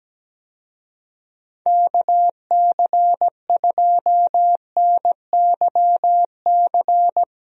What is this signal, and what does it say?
Morse code, heard as a keyed tone, about 17 wpm, 700 Hz. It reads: KC2NYC
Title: USA Call Signs - 6 Characters - Repeated 3x - 17wpm